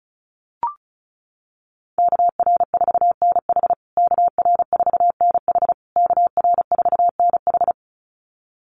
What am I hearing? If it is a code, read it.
Morse code, heard as a keyed tone, about 34 wpm, 700 Hz. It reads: KR4NH KR4NH KR4NH
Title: USA Call Signs Repeated 3x 35wpm